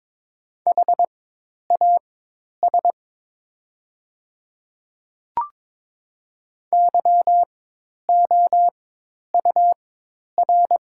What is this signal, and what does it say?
Morse code, heard as a keyed tone, about 22 wpm, spaced as for 10 wpm, 700 Hz. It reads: HAS YOUR
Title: 4x Character Spacing - Top 100 Words - 22wpm